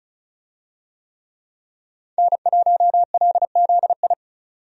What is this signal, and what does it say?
Morse code, heard as a keyed tone, about 36 wpm, 700 Hz. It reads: N1LZI
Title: USA Call Signs - 5 Characters - 35wpm